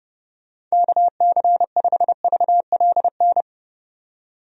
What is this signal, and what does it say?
Morse code, heard as a keyed tone, about 31 wpm, 700 Hz. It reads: KC5VLN
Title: USA Call Signs - 6 Characters - 30wpm